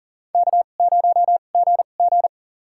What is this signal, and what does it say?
Morse code, heard as a keyed tone, about 40 wpm, spaced as for 29 wpm, 690 Hz. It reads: K0GG
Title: USA Call Signs - 4 Characters - 2x Character Spacing - 40wpm